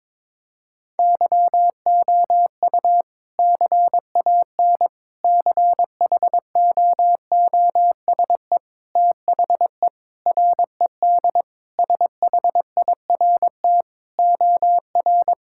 Morse code, heard as a keyed tone, about 22 wpm, 700 Hz. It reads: YOU CAN CHOOSE THE RED SHIRT OR
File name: practice / 22wpm / Sentences with 2-Word Phrases - 22wpm